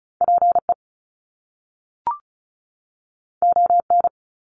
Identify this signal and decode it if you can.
Morse code, heard as a keyed tone, about 36 wpm, 700 Hz. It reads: PE ON
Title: Canadian Provinces and Territories 35wpm